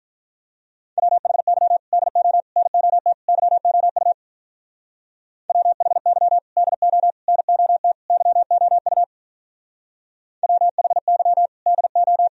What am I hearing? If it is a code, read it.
Morse code, heard as a keyed tone, about 42 wpm, 700 Hz. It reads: WHY DO NOT YOU WHY DO NOT YOU WHY DO
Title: Speed-Racing - Sets of 4 Words 35wpm